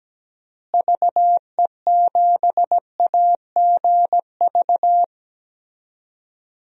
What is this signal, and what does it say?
Morse code, heard as a keyed tone, about 17 wpm, 700 Hz. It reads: VE7AGV
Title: DXCC-100 Call Signs - 6 Characters - Repeated 3x - 17wpm